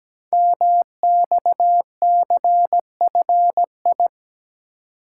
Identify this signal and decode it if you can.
Morse code, heard as a keyed tone, about 17 wpm, 700 Hz. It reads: MXCFI
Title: Code Groups - 5 Letters - 17wpm